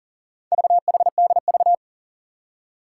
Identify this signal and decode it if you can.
Morse code, heard as a keyed tone, about 41 wpm, 700 Hz. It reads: VHDV
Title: Four Challenging Letter-Number Combinations 40wpm